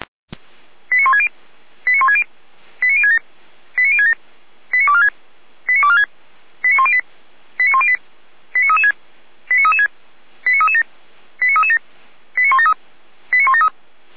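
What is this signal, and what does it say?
Five-tone selective call (ZVEI), heard as quick runs of five tones, tones of about 80 ms. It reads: 89160 89160 89067 89067 89246 89246 89108 89108 89336 89336 89337 89337 89173 89173
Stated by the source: ZVEI five-tone selective call received on an SDR.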